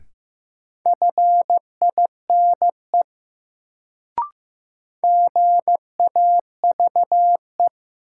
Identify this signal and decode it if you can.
Morse code, heard as a keyed tone, about 15 wpm, 700 Hz. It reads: FINE GAVE
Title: Sentences from Top 500 Words - New Word Drill - 15wpm